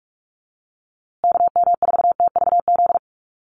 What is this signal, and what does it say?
Morse code, heard as a keyed tone, about 44 wpm, 700 Hz. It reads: KM4TVZ
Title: USA Call Signs Repeated 3x 45wpm